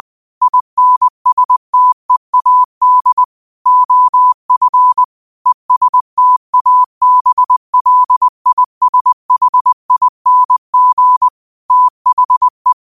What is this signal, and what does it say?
Morse code, heard as a keyed tone, about 20 wpm, 1000 Hz. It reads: INSTEAD OF ESTABLISHING THE
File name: qotdfiles / 20wpm